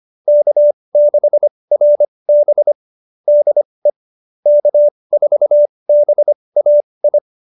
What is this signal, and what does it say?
Morse code, heard as a keyed tone, about 25 wpm, spaced as for 20 wpm, 590 Hz. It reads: K6RB DE K4BAI
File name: QSO301 20